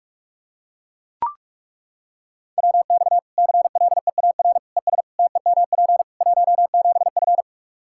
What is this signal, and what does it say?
Morse code, heard as a keyed tone, about 42 wpm, 700 Hz. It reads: WX CLEAR ES TEMP 17F
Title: Practice QSO - Exchange 2 - Info - 45wpm